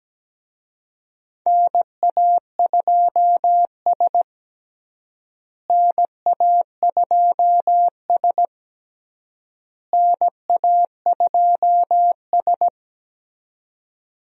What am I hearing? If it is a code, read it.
Morse code, heard as a keyed tone, about 17 wpm, 700 Hz. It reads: NA2S NA2S NA2S
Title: USA Call Signs - 4 Characters - Repeated 3x - 17wpm